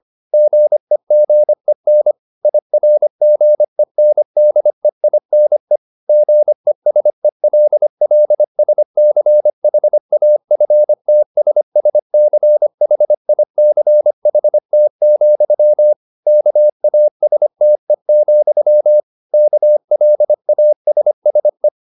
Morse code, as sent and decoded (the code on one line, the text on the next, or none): --. --. -. / .. .-. --. -. -.. .. -. / --. ... .-.. .-.. ... -.-. .... .- ..-. - ... ... -.-. .... .. -.-. .... - --..-- / -.- .- ... - . --..-- / -.- .-.. .- ... ... .
GEGEN IRGENDEINE GESELLSCHAFTSSCHICHT, KASTE, KLASSE